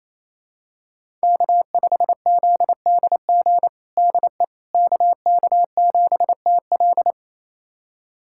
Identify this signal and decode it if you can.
Morse code, heard as a keyed tone, about 28 wpm, 700 Hz. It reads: K5ZDG DE KK7TL
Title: USA Callsign DE Callsign - 5 Characters - 28wpm